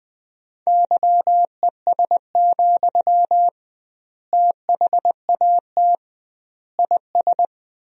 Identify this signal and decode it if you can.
Morse code, heard as a keyed tone, about 21 wpm, 700 Hz. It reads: YES, THAT IS
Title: 2x Word-Spacing - Sentences from Top 500 Words - 20wpm